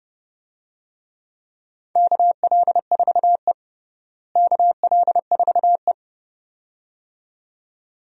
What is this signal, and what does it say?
Morse code, heard as a keyed tone, about 30 wpm, 700 Hz. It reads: KL4E KL4E
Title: USA Call Signs - 4 Characters - Repeated 2x - 30wpm